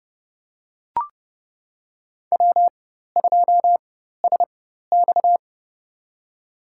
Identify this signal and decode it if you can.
Morse code, heard as a keyed tone, about 30 wpm, spaced as for 14 wpm, 700 Hz. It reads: W2SX
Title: USA Call Signs - 4 Characters - 4x Character Spacing - 30wpm